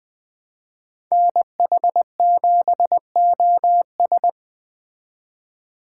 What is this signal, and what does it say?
Morse code, heard as a keyed tone, about 20 wpm, 700 Hz. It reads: NH7OS
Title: Speed Racing - USA 2x2 Call Signs 17wpm